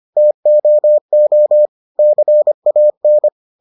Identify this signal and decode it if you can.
Morse code, heard as a keyed tone, about 25 wpm, 600 Hz. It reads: TOO CAN